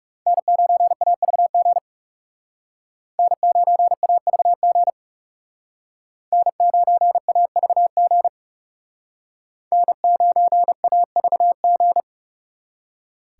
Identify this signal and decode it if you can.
Morse code, heard as a keyed tone, about 38 wpm, 700 Hz. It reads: N9AVG N9AVG N9AVG N9AVG